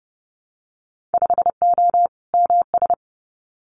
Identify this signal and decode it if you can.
Morse code, heard as a keyed tone, about 30 wpm, 700 Hz. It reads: 5O MS
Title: Winter Field Day Exchanges Encore 30wpm